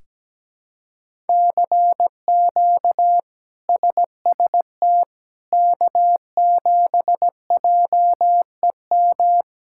Morse code, heard as a keyed tone, about 17 wpm, 700 Hz. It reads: CQ SST K7JEM
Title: K1USN SST QSO Practice - 17wpm